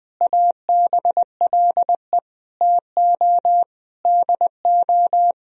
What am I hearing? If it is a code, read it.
Morse code, heard as a keyed tone, about 20 wpm, 700 Hz. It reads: ABLE TO DO
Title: Sentences from Top 1000 Words - Review Part 2 - 20wpm